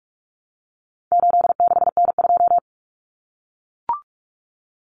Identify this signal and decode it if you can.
Morse code, heard as a keyed tone, about 44 wpm, 700 Hz. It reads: 86N2
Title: Common Cut Numbers - 4-Digit 45wpm